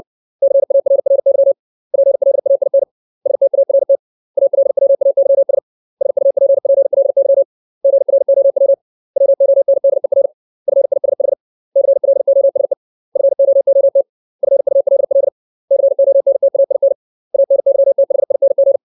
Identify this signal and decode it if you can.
Morse code, heard as a keyed tone, about 60 wpm, 550 Hz. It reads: CARRY PLAIN START ALWAYS SUPPLY GROW WONDER FISH CLOSE FOOT FULL CONTAIN ANYTHING
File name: 632 words / 632 Words 60WPM